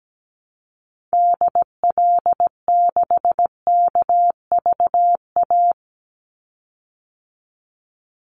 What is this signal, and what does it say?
Morse code, heard as a keyed tone, about 17 wpm, 700 Hz. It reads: DL6KVA
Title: Call Signs Repeated 2x - 17wpm